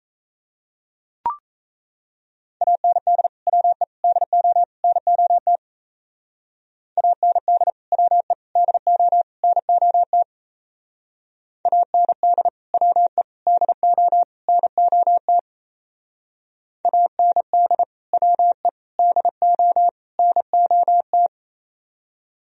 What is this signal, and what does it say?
Morse code, heard as a keyed tone, about 34 wpm, 700 Hz. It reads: AND WE DO NOT AND WE DO NOT AND WE DO NOT AND WE DO NOT
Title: Speed-Racing - Sets of 4 Words 28wpm